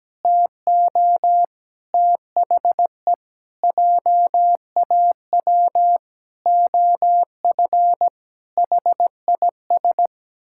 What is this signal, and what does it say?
Morse code, heard as a keyed tone, about 17 wpm, 700 Hz. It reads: TO THE JAW OF HIS
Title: Pangram Challenge v2 - 17wpm